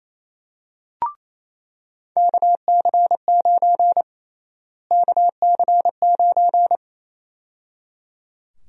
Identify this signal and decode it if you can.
Morse code, heard as a keyed tone, about 28 wpm, 700 Hz. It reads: KC9 KC9
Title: USA Call Sign Prefixes - Repeated 2x - 28wpm